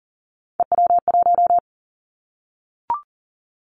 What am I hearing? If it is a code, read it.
Morse code, heard as a keyed tone, about 41 wpm, 700 Hz. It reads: EW1
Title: Callsign Prefixes for Top 50 DXCC - 40wpm